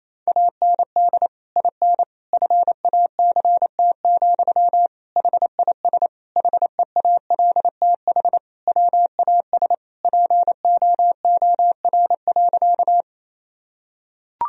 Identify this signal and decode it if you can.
Morse code, heard as a keyed tone, about 28 wpm, 700 Hz. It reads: AND IN FACT, HIS HEALTH WAS POOR.
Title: Sentences from Top 1000 Words - Review Part 3 - 28wpm